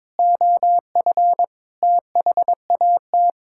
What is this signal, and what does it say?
Morse code, heard as a keyed tone, about 22 wpm, 700 Hz. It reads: OF THAT
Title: Sentences from Top 1000 Words - Review Part 5 - 22wpm